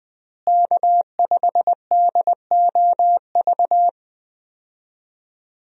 Morse code, heard as a keyed tone, about 21 wpm, 700 Hz. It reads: K5DOV
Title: USA Call Signs - 5 Characters - Repeated 3x - 20wpm